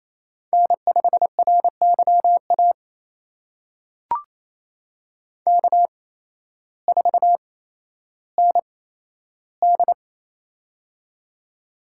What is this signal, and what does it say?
Morse code, heard as a keyed tone, about 28 wpm, spaced as for 13 wpm, 700 Hz. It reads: N5RYA K4ND
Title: USA Call Signs - 8x Character Spacing - 28wpm